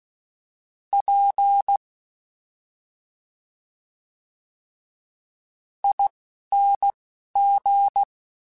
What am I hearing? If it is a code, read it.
Morse code, heard as a keyed tone, about 16 wpm, spaced as for 5 wpm, 800 Hz. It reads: PING
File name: Special INGs-mp3s / Special INGs-mp3s / Slow ing @10